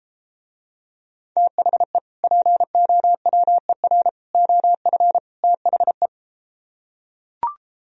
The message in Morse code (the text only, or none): THE POWER OF THE